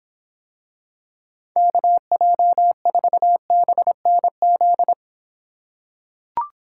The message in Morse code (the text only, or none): KJ4BNZ